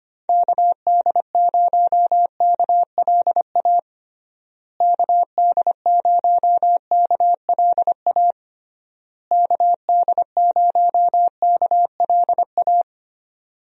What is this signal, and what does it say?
Morse code, heard as a keyed tone, about 25 wpm, 700 Hz. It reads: KD0KLA KD0KLA KD0KLA
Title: USA Call Signs - 6 Characters - Repeated 3x - 25wpm